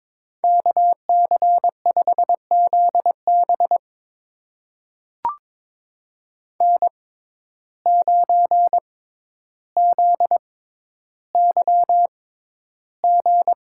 Morse code, heard as a keyed tone, about 23 wpm, spaced as for 11 wpm, 700 Hz. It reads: KC5ZB N9ZYG
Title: USA Call Signs - 5 Characters - 6x Character Spacing - 22wpm